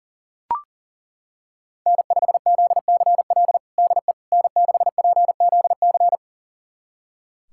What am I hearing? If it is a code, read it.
Morse code, heard as a keyed tone, about 40 wpm, 700 Hz. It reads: N5ZCL DE N6PZC
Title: USA Callsign DE Callsign 40wpm